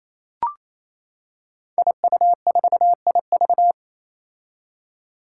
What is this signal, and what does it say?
Morse code, heard as a keyed tone, about 27 wpm, 700 Hz. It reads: IU4IV
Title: Five Challenging Letter-Number Combinations 28wpm